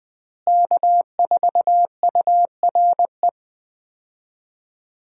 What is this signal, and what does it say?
Morse code, heard as a keyed tone, about 20 wpm, 700 Hz. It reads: K4URE
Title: USA Call Signs - 5 Characters - Repeated 3x - 20wpm